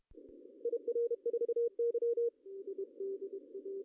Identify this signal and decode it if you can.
Morse code, heard as a keyed tone, about 32 wpm, 470 Hz. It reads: IR4Y
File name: CW / cqww-cw-k5tr